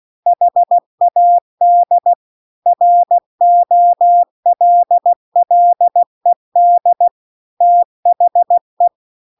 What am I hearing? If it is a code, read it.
Morse code, heard as a keyed tone, about 16 wpm, 700 Hz. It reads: HAD ROLLED THE